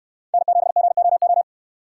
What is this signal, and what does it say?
Morse code, heard as a keyed tone, about 69 wpm, 700 Hz. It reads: N7GOO